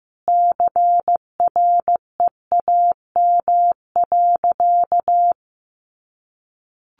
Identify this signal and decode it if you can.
Morse code, heard as a keyed tone, about 14 wpm, 700 Hz. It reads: CREAM.
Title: Sentences from Top 1500 Words 15wpm